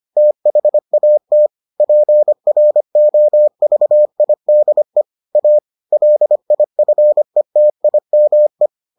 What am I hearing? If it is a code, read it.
Morse code, heard as a keyed tone, about 25 wpm, 600 Hz. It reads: THAT PROVIDE A LIFETIME